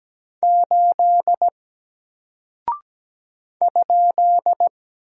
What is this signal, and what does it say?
Morse code, heard as a keyed tone, about 17 wpm, 700 Hz. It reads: 8 ?
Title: Single Character (Letter:Number:Punctuation:Prosign) Alt 17wpm